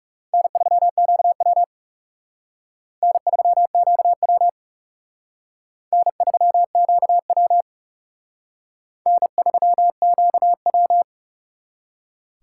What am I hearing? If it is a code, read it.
Morse code, heard as a keyed tone, about 35 wpm, 700 Hz. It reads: N3QW N3QW N3QW N3QW
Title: Speed Racing - USA 1x2 Call Signs 30wpm